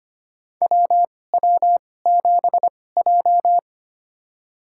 Morse code, heard as a keyed tone, about 25 wpm, spaced as for 18 wpm, 700 Hz. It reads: WW7J
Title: USA Call Signs - 2x Character Spacing - 25wpm